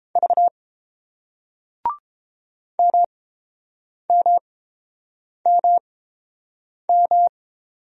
Morse code, heard as a keyed tone, about 30 wpm, spaced as for 7 wpm, 700 Hz. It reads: V MMMM